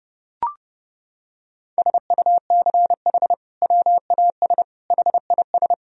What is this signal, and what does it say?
Morse code, heard as a keyed tone, about 29 wpm, 700 Hz. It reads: SUCH WAS HIS
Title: Sentences from Top 1000 Words - Review Part 3 - 30wpm